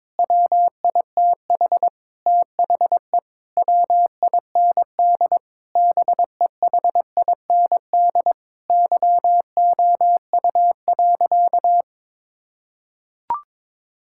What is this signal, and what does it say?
Morse code, heard as a keyed tone, about 22 wpm, 700 Hz. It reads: WITH THE WIND BEHIND YOU.
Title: Sentences from Top 1000 Words - Review Part 6 - 22wpm